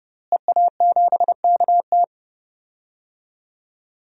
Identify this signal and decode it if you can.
Morse code, heard as a keyed tone, about 32 wpm, 700 Hz. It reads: EA7KT
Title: DXCC-100 Call Signs - 5 Characters - Repeated 3x - 30wpm